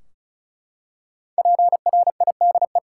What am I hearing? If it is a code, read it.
Morse code, heard as a keyed tone, about 34 wpm, 700 Hz. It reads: PRIDE
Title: Five Letter Words - 35wpm